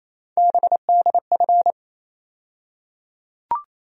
Morse code, heard as a keyed tone, about 27 wpm, 700 Hz. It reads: BDF
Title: USA Call Sign Postfixes - 3 Letters 28wpm